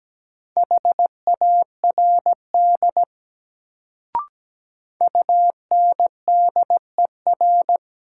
Morse code, heard as a keyed tone, about 17 wpm, 700 Hz. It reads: HARD UNDER
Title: Sentences from Top 300 Words - New Word Drill - 17wpm